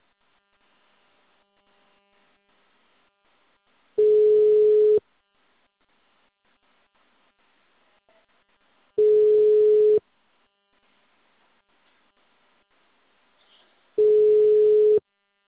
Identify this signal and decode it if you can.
Ringback tone (Europe, 425 Hz).
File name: out- -2011- - - .1951